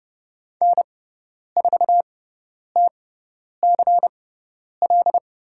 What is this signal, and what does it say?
Morse code, heard as a keyed tone, about 30 wpm, spaced as for 10 wpm, 700 Hz. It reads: N4TCL
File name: Lesson 16g - Letter c - Call Signs 30wpm